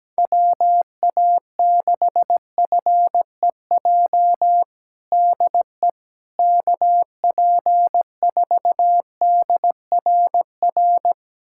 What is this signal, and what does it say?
Morse code, heard as a keyed tone, about 17 wpm, 700 Hz. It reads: WA6FEJ DE KP4DRR
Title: USA Callsign DE Callsign - 6 Characters - 17wpm